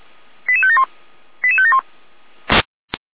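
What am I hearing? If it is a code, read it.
Five-tone selective call (ZVEI), heard as quick runs of five tones, tones of about 70 ms. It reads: 88571 88571
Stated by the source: ZVEI five-tone selective call received on an SDR.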